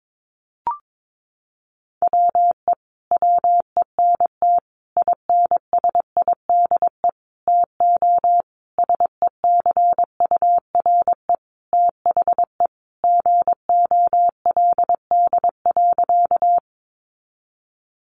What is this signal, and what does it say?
Morse code, heard as a keyed tone, about 22 wpm, 700 Hz. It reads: WE WENT INSIDE TO SECURE THE GOLD.
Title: Sentences from Top 1200 Words 22wpm